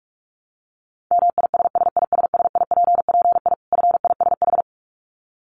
Morse code, heard as a keyed tone, about 44 wpm, 700 Hz. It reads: MISSISSIPPI FISH
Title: Dit Challenge 45wpm